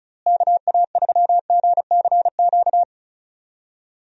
Morse code, heard as a keyed tone, about 34 wpm, 700 Hz. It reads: KA3GCQ
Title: USA Call Signs - 6 Characters - Repeated 3x - 35wpm